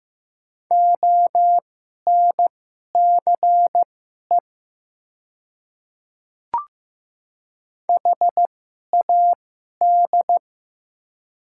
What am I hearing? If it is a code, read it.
Morse code, heard as a keyed tone, about 15 wpm, spaced as for 11 wpm, 700 Hz. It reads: ONCE HAD
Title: 2x Character Spacing - Top 300 Words - 15wpm